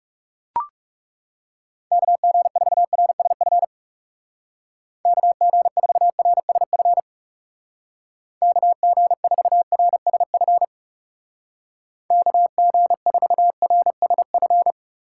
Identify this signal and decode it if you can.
Morse code, heard as a keyed tone, about 36 wpm, 700 Hz. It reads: KG4RSF KG4RSF KG4RSF KG4RSF
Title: Speed Racing - USA 2x3 Call Signs 30wpm